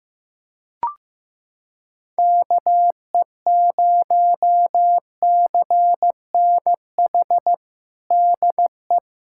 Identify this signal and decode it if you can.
Morse code, heard as a keyed tone, about 15 wpm, 700 Hz. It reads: KE0CNH DE
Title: USA Callsign DE Callsign - 6 Characters - 15wpm